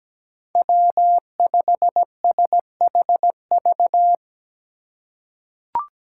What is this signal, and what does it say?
Morse code, heard as a keyed tone, about 17 wpm, 700 Hz. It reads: W5SHV
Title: USA Call Signs - 5 Characters - 3x Character Spacing - 17wpm